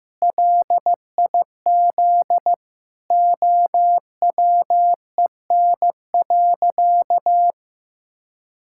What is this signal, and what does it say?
Morse code, heard as a keyed tone, about 15 wpm, 700 Hz. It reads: LIZ OWEN.
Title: Pangram Challenge v2 - 15wpm